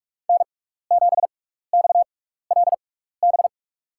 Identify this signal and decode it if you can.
Morse code, heard as a keyed tone, about 43 wpm, spaced as for 15 wpm, 700 Hz. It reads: N7XLB
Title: USA Call Signs - 5 Characters - 6x Character Spacing - 45wpm